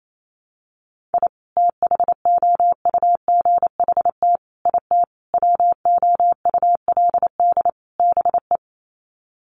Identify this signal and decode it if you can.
Morse code, heard as a keyed tone, about 28 wpm, 700 Hz. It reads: I THOUGHT IT WOULD BE
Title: Sets of 5 Words Encore 28wpm